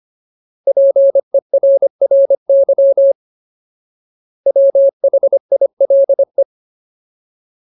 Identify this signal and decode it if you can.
Morse code, heard as a keyed tone, about 25 wpm, 550 Hz. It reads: PERRY WHILE
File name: Ultimate Word List / 25WPM Ultimate